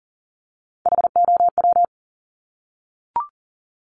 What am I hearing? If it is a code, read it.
Morse code, heard as a keyed tone, about 40 wpm, 700 Hz. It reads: HOW
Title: Multi-letter Phonemes Repeated 3x 40wpm